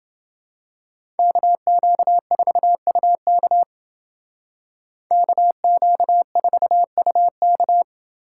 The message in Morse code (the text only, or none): KQ4UK KQ4UK